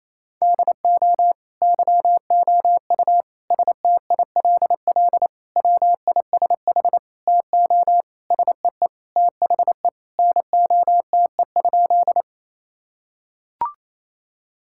Morse code, heard as a keyed tone, about 28 wpm, 700 Hz. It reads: DO YOU STILL WISH TO SEE THE NOTE?
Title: Sentences from Top 1600 Words 28wpm